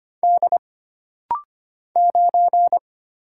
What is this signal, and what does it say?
Morse code, heard as a keyed tone, about 25 wpm, 700 Hz. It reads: D 9
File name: Single Character (Letter:Number:Punctuation:Prosign) - Rapid-Fire - 25wpm